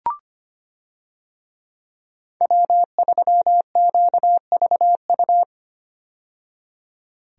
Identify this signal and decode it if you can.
Morse code, heard as a keyed tone, about 25 wpm, 700 Hz. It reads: W3QVU
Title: USA Call Signs - 5 Characters - 25wpm